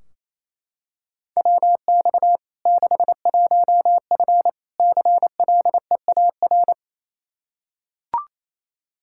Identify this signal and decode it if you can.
Morse code, heard as a keyed tone, about 28 wpm, 700 Hz. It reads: WX 61F CLEAR
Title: K1USN SST QSO Practice 28wpm